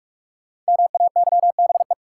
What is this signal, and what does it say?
Morse code, heard as a keyed tone, about 42 wpm, 700 Hz. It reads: MAYBE
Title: Sentences from Top 1000 Words - New Word Drill - 45wpm